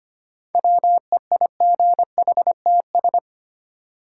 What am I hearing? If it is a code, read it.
Morse code, heard as a keyed tone, about 25 wpm, 700 Hz. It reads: WEIGHTS
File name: practice / 25wpm / Easy Words with Hints - 25wpm